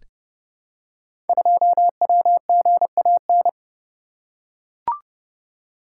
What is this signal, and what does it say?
Morse code, heard as a keyed tone, about 30 wpm, 700 Hz. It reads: 2WGAN
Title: Code Groups - 5 Letter:Numbers - 30wpm